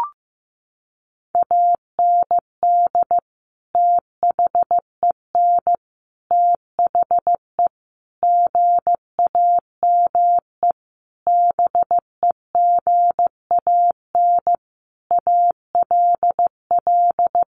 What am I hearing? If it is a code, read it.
Morse code, heard as a keyed tone, about 15 wpm, 700 Hz. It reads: AND THEN THE GAME BEGAN ALL
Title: Sentences from Top 500 Words 15wpm